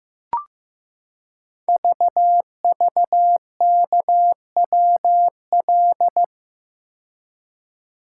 Morse code, heard as a keyed tone, about 15 wpm, 700 Hz. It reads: VVKWL
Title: Code Groups - 5 Letters - 15wpm